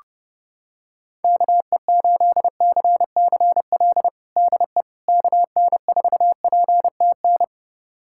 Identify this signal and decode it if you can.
Morse code, heard as a keyed tone, about 30 wpm, 700 Hz. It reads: KE8CCL DE KN4PTN
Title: USA Callsign DE Callsign - 6 Characters - 30wpm